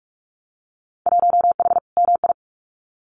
Morse code, heard as a keyed tone, about 46 wpm, 700 Hz. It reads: 1H MI
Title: Winter Field Day Exchanges 45wpm